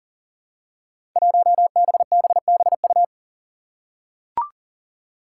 Morse code, heard as a keyed tone, about 40 wpm, 700 Hz. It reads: 1BBBU
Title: Five Challenging Letter-Number Combinations 40wpm